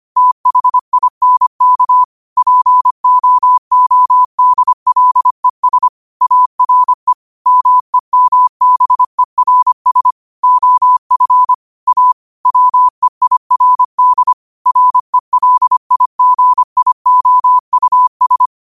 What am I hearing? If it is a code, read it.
Morse code, heard as a keyed tone, about 25 wpm, 1000 Hz. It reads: THINK POODLES ARE MEMBERS OF A WEIRD RELIGIOUS